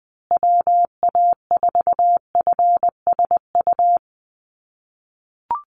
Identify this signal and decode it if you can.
Morse code, heard as a keyed tone, about 20 wpm, 700 Hz. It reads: WA4FSU
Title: USA Call Signs Repeated 2x 20wpm